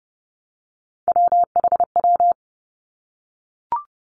Morse code, heard as a keyed tone, about 30 wpm, 700 Hz. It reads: WHW
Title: Three Challenging Letter-Number Combinations 30wpm